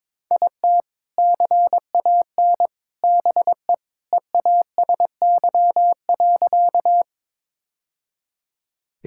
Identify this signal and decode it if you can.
Morse code, heard as a keyed tone, about 22 wpm, 700 Hz. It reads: IT CAN BE EASY.
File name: practice / 22wpm / Sentences with 2-Word Phrases - 22wpm